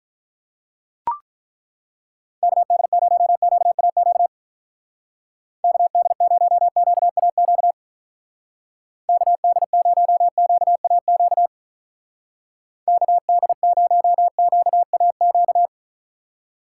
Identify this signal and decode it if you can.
Morse code, heard as a keyed tone, about 43 wpm, 700 Hz. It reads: KD0QAQ KD0QAQ KD0QAQ KD0QAQ
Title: Speed Racing - USA 2x3 Call Signs 35wpm